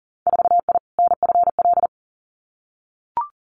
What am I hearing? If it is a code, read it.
Morse code, heard as a keyed tone, about 40 wpm, 700 Hz. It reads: 4I NFL
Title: Winter Field Day Exchanges 40wpm